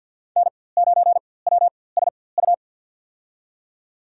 Morse code, heard as a keyed tone, about 52 wpm, spaced as for 24 wpm, 700 Hz. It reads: N9WSU
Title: USA Call Signs - 5 Characters - 4x Character Spacing - 50wpm